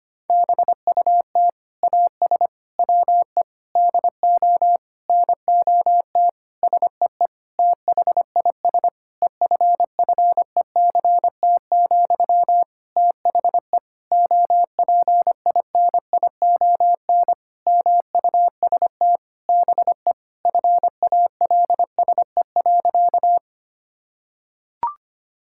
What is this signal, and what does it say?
Morse code, heard as a keyed tone, about 25 wpm, 700 Hz. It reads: BUT AS WE DO NOT SEE THIS EFFECT, THE OPINION MUST BE FALSE.
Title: Sentences from Top 1600 Words 25wpm